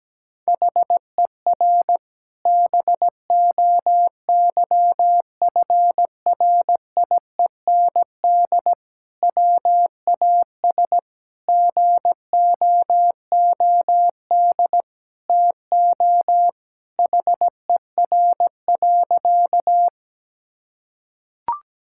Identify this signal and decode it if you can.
Morse code, heard as a keyed tone, about 17 wpm, 700 Hz. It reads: HER BOYFRIEND WAS GOOD TO HER.
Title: Sentences from Top 1000 Words - Review Part 5 - 17wpm